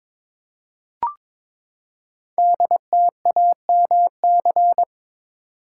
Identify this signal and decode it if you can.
Morse code, heard as a keyed tone, about 22 wpm, 700 Hz. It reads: DTAMC
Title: Code Groups - 5 Letters - 22wpm